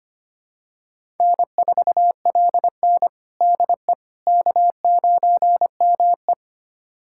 Morse code, heard as a keyed tone, about 25 wpm, 700 Hz. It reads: N4LN DE K9ME
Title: USA Callsign DE Callsign - 4 Characters - 25wpm